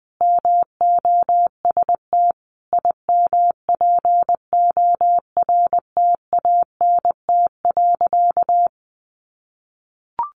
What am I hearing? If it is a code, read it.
Morse code, heard as a keyed tone, about 20 wpm, 700 Hz. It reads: MOST IMPORTANT.
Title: Sentences from Top 1300 Words 20wpm